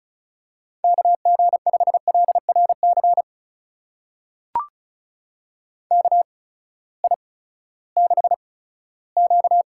Morse code, heard as a keyed tone, about 36 wpm, spaced as for 18 wpm, 700 Hz. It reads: KG5LRC KI6Q
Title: USA Call Signs - 6 Characters - 8x Character Spacing - 35wpm